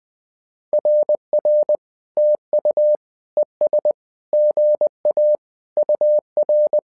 Morse code, heard as a keyed tone, about 20 wpm, 600 Hz. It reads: RR TU ES GA UR
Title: SOTA QSO Practice 20wpm